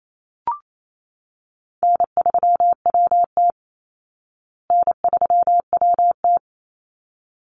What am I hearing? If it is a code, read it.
Morse code, heard as a keyed tone, about 28 wpm, 700 Hz. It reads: N3WT N3WT
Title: Easy Call Signs Repeated 2x - 28wpm